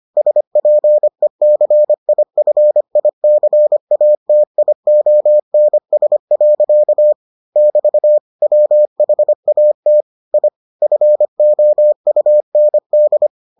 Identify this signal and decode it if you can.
Morse code, heard as a keyed tone, about 25 wpm, 600 Hz. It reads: SPECIFICATIONS. <BT> WHAT I FOUND